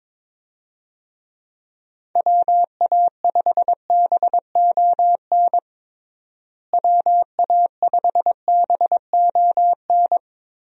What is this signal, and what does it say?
Morse code, heard as a keyed tone, about 22 wpm, 700 Hz. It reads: WA5BON WA5BON
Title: USA Call Signs - 6 Characters - Repeated 2x - 22wpm